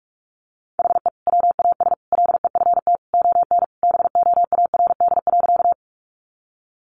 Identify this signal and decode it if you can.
Morse code, heard as a keyed tone, about 44 wpm, 700 Hz. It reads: HE WAS LEFT ON BOARD.
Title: Sentences from Top 700 Words 45wpm